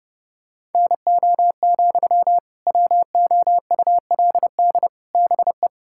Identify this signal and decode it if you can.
Morse code, heard as a keyed tone, about 30 wpm, 700 Hz. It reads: NO, WOULD BE
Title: Sentences from Top 1600 Words 30wpm